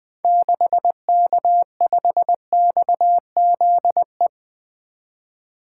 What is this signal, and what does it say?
Morse code, heard as a keyed tone, about 20 wpm, 700 Hz. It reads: 6K5XZE
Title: 2023 SOTA Activator Call Signs - 20wpm